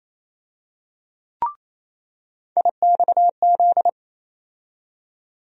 Morse code, heard as a keyed tone, about 28 wpm, 700 Hz. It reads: IXZ
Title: USA Call Sign Postfixes - 3 Letters 28wpm